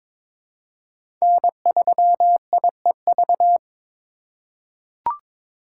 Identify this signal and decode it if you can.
Morse code, heard as a keyed tone, about 22 wpm, 700 Hz. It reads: N3IEV
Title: USA Call Signs - 5 Characters - 2x Character Spacing - 22wpm